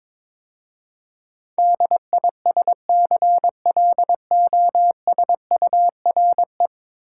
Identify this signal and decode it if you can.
Morse code, heard as a keyed tone, about 22 wpm, 700 Hz. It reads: DISCLOSURE
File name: practice / 22wpm / 10 Letter Words - First 3 Letters Given - 22wpm